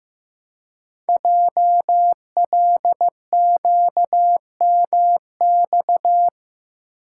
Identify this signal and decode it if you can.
Morse code, heard as a keyed tone, about 15 wpm, 700 Hz. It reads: JLQMX
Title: Code Groups - 5 Letters - 15wpm